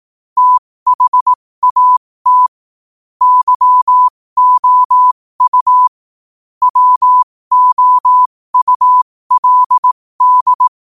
Morse code, heard as a keyed tone, about 18 wpm, spaced as for 15 wpm, 1000 Hz. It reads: THAT YOU WOULD
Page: New quotes every day in morse code at 15 Words per minute.